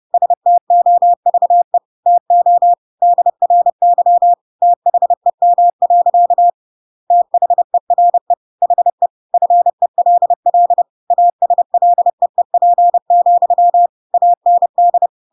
Morse code, heard as a keyed tone, about 30 wpm, 700 Hz. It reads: STOVE TO DRY THEM. THERE HE FELL ASLEEP, AND
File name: Pinocchio / Pinocchio ch6 30 30 ik1hge